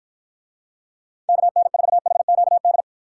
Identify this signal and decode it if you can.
Morse code, heard as a keyed tone, about 53 wpm, 700 Hz. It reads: KN4HQD